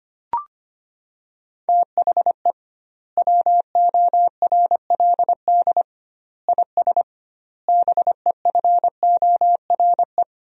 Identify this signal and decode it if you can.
Morse code, heard as a keyed tone, about 25 wpm, 700 Hz. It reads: THE WORLD IS BEFORE
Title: 2x Word-Spacing - Sentences from Top 200 Words - 25wpm